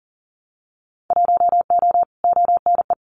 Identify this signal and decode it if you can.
Morse code, heard as a keyed tone, about 40 wpm, 700 Hz. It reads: 1O ONE
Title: Winter Field Day Exchanges 40wpm